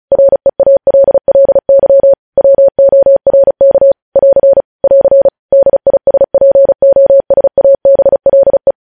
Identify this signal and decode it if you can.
Morse code, heard as a keyed tone, about 36 wpm, 550 Hz. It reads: REALLY WORK <AR> <AR> DISPOSABLE
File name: podcast / News headlines-1x-35x35